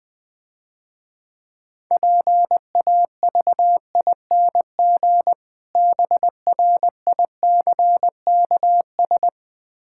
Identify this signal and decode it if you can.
Morse code, heard as a keyed tone, about 20 wpm, 700 Hz. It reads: PAVING BRICKS
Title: Difficult Words - Speed-Racing with Hints - 20wpm